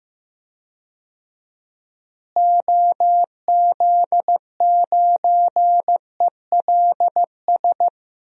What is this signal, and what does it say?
Morse code, heard as a keyed tone, about 15 wpm, 700 Hz. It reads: OZ9ELS
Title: DXCC-100 Call Signs - 6 Characters - Repeated 2x - 15wpm